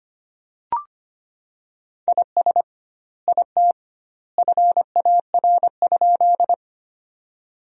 Morse code, heard as a keyed tone, about 25 wpm, 700 Hz. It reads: IS IT FAR?
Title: 2x Word-Spacing - Sentences from Top 300 Words - 25wpm